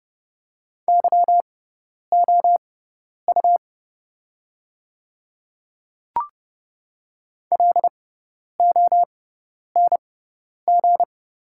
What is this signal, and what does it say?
Morse code, heard as a keyed tone, about 30 wpm, spaced as for 10 wpm, 700 Hz. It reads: YOU LONG
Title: 6x Character Spacing - Top 200 Words - 30wpm